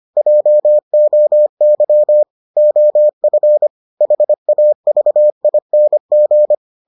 Morse code, heard as a keyed tone, about 25 wpm, 600 Hz. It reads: JOY OF HAVING